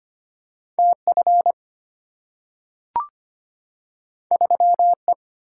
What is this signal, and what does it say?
Morse code, heard as a keyed tone, about 25 wpm, 700 Hz. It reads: TF 3E
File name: Two Letter-Number Combinations - 25wpm